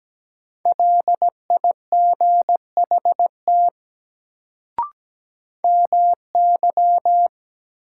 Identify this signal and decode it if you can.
Morse code, heard as a keyed tone, about 17 wpm, 700 Hz. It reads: LIGHT MY
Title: Sentences from Top 200 Words - New Word Drill - 17wpm